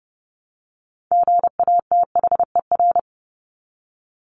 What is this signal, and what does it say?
Morse code, heard as a keyed tone, about 30 wpm, 700 Hz. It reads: GATHER
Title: Top 1501-1750 Words 30wpm